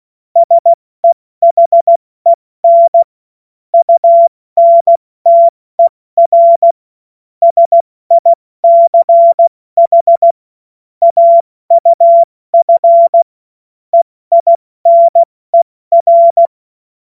Morse code, heard as a keyed tone, about 16 wpm, 680 Hz. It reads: SEHEN UNTER SICH AUF EINER
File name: Morsekurs / Witze / Witz 16. August 70 F80